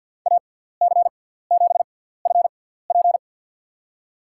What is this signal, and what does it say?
Morse code, heard as a keyed tone, about 52 wpm, spaced as for 17 wpm, 700 Hz. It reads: AC7FP
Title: USA Call Signs - 5 Characters - 6x Character Spacing - 50wpm